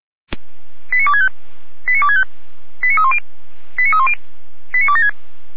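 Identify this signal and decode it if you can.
Five-tone selective call (ZVEI), heard as quick runs of five tones, tones of about 80 ms. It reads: 89265 89265 89310 89310 89276
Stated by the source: ZVEI five-tone selective call received on an SDR.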